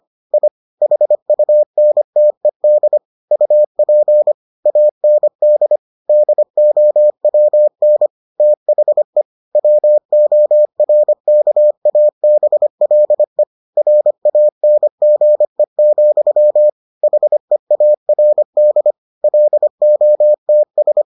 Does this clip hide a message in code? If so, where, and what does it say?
Yes: Morse code from 0.3 to 21.0 s, heard as a keyed tone, about 25 wpm, 600 Hz. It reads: I HUNTED UP AND DOWN THE WORKABLE RANGE, HEARD LOTS